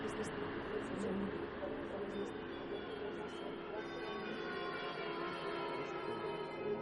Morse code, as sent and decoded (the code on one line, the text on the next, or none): none